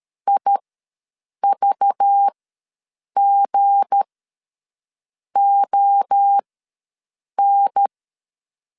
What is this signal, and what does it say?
Morse code, heard as a keyed tone, about 13 wpm, spaced as for 6 wpm, 800 Hz. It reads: IVGON
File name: LEZIONI / 8 YJ45 / 8 Y ws 12wpm 3409